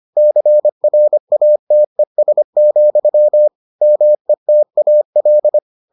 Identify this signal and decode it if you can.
Morse code, heard as a keyed tone, about 25 wpm, 600 Hz. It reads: CRATES, METAL